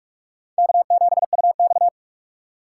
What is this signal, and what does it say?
Morse code, heard as a keyed tone, about 39 wpm, 700 Hz. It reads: K7UX